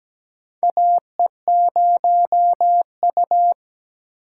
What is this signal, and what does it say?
Morse code, heard as a keyed tone, about 17 wpm, 700 Hz. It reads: AE0U
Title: USA Call Signs - 4 Characters - 4x Character Spacing - 17wpm